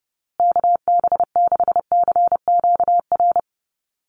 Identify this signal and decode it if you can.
Morse code, heard as a keyed tone, about 30 wpm, 700 Hz. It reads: KB6CQR
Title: USA Call Signs Repeated 3x 30wpm